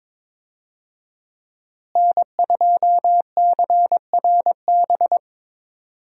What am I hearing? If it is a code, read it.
Morse code, heard as a keyed tone, about 22 wpm, 700 Hz. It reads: N2CRB
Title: USA Call Signs - 5 Characters - Repeated 2x - 22wpm